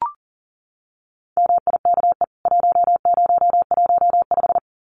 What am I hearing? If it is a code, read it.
Morse code, heard as a keyed tone, about 40 wpm, 700 Hz. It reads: MIKE 1015
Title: CWT Exchange 40wpm